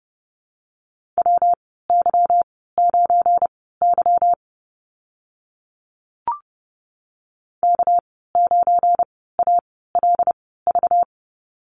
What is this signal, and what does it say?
Morse code, heard as a keyed tone, about 30 wpm, spaced as for 17 wpm, 700 Hz. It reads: WY9Y K9ALV
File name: Lesson 37g - Letter x - Call Signs 30wpm